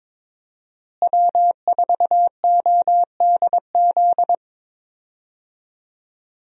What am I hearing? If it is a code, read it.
Morse code, heard as a keyed tone, about 22 wpm, 700 Hz. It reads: W4ODZ
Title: USA Call Signs - 5 Characters - Repeated 3x - 22wpm